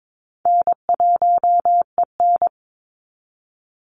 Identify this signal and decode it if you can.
Morse code, heard as a keyed tone, about 22 wpm, 700 Hz. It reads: N1EN
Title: Call Signs Repeated 3x - 22wpm